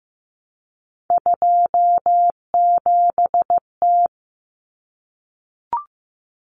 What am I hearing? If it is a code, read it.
Morse code, heard as a keyed tone, about 15 wpm, 700 Hz. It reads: 27T
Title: Common Cut Numbers - 3-Digit 15wpm